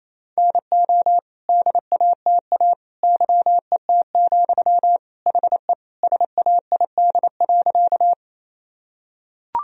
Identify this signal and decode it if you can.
Morse code, heard as a keyed tone, about 28 wpm, 700 Hz. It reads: NO DATA YET, HE SAID.
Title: Sentences from Top 1000 Words - Review Part 3 - 28wpm